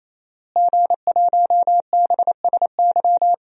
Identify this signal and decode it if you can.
Morse code, heard as a keyed tone, about 28 wpm, 700 Hz. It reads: G1BSY
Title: DXCC-100 Call Signs - 5 Characters - Repeated 3x - 28wpm